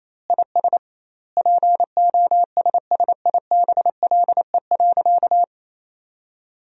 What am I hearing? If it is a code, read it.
Morse code, heard as a keyed tone, about 28 wpm, 700 Hz. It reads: IS POSSIBLE.
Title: 2x Word-Spacing - Sentences from Top 500 Words - 28wpm